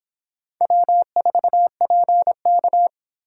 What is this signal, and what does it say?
Morse code, heard as a keyed tone, about 30 wpm, 700 Hz. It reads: W4PK